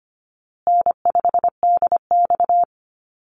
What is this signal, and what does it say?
Morse code, heard as a keyed tone, about 25 wpm, 700 Hz. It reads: N5DX
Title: Easy Call Signs Repeated 3x - 25wpm